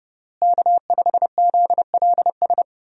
Morse code, heard as a keyed tone, about 32 wpm, 700 Hz. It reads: K5ZLS
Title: USA Call Signs - 5 Characters - Repeated 3x - 30wpm